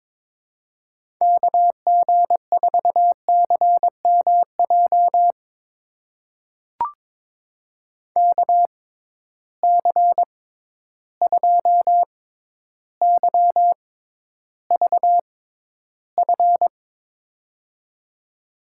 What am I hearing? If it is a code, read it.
Morse code, heard as a keyed tone, about 22 wpm, spaced as for 11 wpm, 700 Hz. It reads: KG4CMJ KC2YVF
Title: USA Call Signs - 6 Characters - 6x Character Spacing - 22wpm